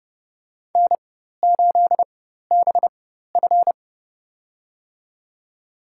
Morse code, heard as a keyed tone, about 29 wpm, spaced as for 14 wpm, 700 Hz. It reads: N8BF
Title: USA Call Signs - 4x Character Spacing - 30wpm